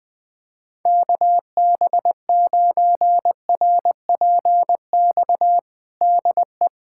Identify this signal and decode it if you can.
Morse code, heard as a keyed tone, about 20 wpm, 700 Hz. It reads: KB9RPX DE
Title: USA Callsign DE Callsign - 6 Characters - 20wpm